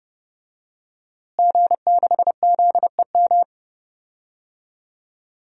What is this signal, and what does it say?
Morse code, heard as a keyed tone, about 32 wpm, 700 Hz. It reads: G6ZEM
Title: DXCC-100 Call Signs - 5 Characters - Repeated 3x - 30wpm